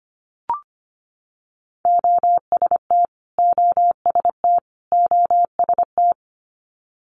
Morse code, heard as a keyed tone, about 25 wpm, 700 Hz. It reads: OST OST OST
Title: Multi-letter Phonemes Repeated 3x 25wpm